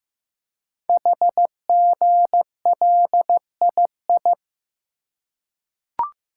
Morse code, heard as a keyed tone, about 15 wpm, 700 Hz. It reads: HGLII
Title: Code Groups - 5 Letters - 15wpm